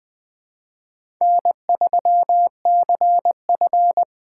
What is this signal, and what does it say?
Morse code, heard as a keyed tone, about 20 wpm, 700 Hz. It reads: N3CF
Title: USA Call Signs - 4 Characters - 2x Character Spacing - 20wpm